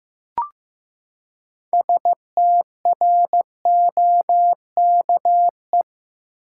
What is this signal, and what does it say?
Morse code, heard as a keyed tone, about 15 wpm, 700 Hz. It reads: STROKE
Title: Six Letter Words - 15wpm